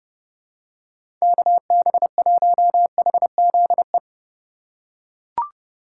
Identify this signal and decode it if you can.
Morse code, heard as a keyed tone, about 31 wpm, 700 Hz. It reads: KB1HZE
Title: USA Call Signs - 6 Characters - 2x Character Spacing - 30wpm